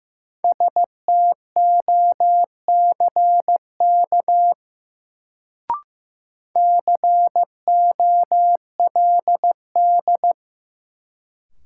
Morse code, heard as a keyed tone, about 15 wpm, 700 Hz. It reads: STOCK COLD
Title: Sets of 5 Words - New Word Drill - 15wpm